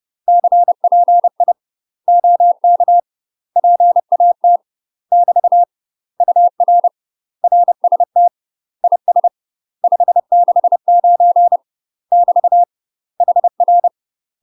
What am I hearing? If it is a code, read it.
Morse code, heard as a keyed tone, about 30 wpm, 700 Hz. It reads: CPI OK PAT <BT> UR RST IS 569 <BT> HR
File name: UK MASTER QSOs TRAINING / 30WPM / UK MASTER QSOs 0007